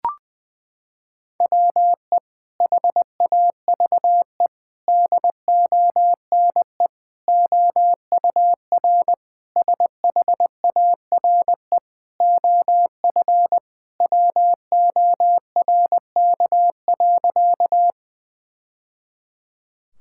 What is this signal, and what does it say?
Morse code, heard as a keyed tone, about 21 wpm, 700 Hz. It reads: WE HAVE DONE OUR SHARE OF WORK.
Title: Sentences from Top 1000 Words - Review Part 6 - 20wpm